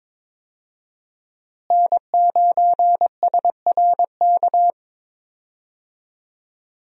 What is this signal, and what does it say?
Morse code, heard as a keyed tone, about 22 wpm, 700 Hz. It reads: N9SRK
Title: USA Call Signs - 5 Characters - Repeated 2x - 22wpm